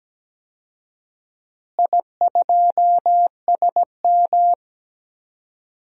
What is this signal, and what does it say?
Morse code, heard as a keyed tone, about 17 wpm, 700 Hz. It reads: I2SM
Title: DXCC-100 Call Signs - 4 Characters - Repeated 3x - 17wpm